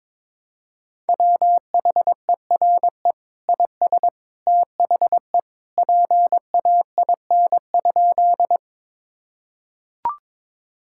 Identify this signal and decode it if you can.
Morse code, heard as a keyed tone, about 22 wpm, 700 Hz. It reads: WHERE IS THE PAIN?
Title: Sentences from Top 1600 Words 22wpm